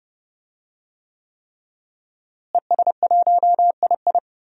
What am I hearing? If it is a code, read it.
Morse code, heard as a keyed tone, about 30 wpm, 700 Hz. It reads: ES1II
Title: DXCC-100 Call Signs - 5 Characters - Repeated 3x - 30wpm